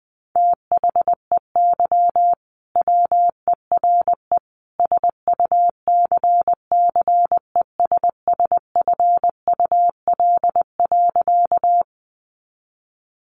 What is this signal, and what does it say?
Morse code, heard as a keyed tone, about 20 wpm, 700 Hz. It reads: THEY WERE SUCCESSFUL.
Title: Sentences from Top 900 Words 20wpm